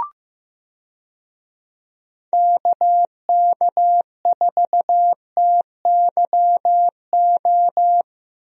Morse code, heard as a keyed tone, about 15 wpm, 700 Hz. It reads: KK4TYO
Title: USA Call Signs - 6 Characters - Repeated 3x - 15wpm